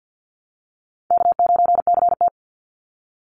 Morse code, heard as a keyed tone, about 48 wpm, 700 Hz. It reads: K9CT
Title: Call Signs Repeated 3x - 50wpm